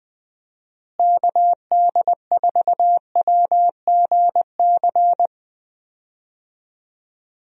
Morse code, heard as a keyed tone, about 20 wpm, 700 Hz. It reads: KD4WGC
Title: USA Call Signs - 6 Characters - Repeated 3x - 20wpm